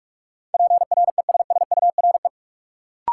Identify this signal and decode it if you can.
Morse code, heard as a keyed tone, about 46 wpm, 700 Hz. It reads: PRESSURE
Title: Sentences from Top 1400 Words - New Word Drill Alt - 45wpm